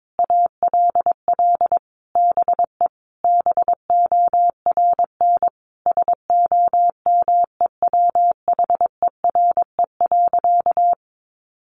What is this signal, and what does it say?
Morse code, heard as a keyed tone, about 22 wpm, 700 Hz. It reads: ALL BE BORN SOMEWHERE.
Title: Sentences from Top 1400 Words 22wpm